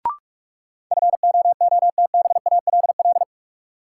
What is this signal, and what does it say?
Morse code, heard as a keyed tone, about 46 wpm, 700 Hz. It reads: FOOTBALL
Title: Sentences from Top 1600 Words - New Word Drill Alt - 45wpm